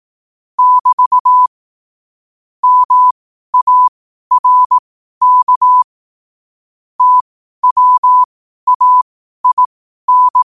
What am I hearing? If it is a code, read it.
Morse code, heard as a keyed tone, about 18 wpm, spaced as for 12 wpm, 1000 Hz. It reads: <BT> MARK TWAIN
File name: qotdfiles / 12wpm